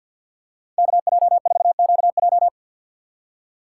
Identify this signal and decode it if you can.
Morse code, heard as a keyed tone, about 51 wpm, 700 Hz. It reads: KJ4QJ
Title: USA Call Signs - 6x Character Spacing - 50wpm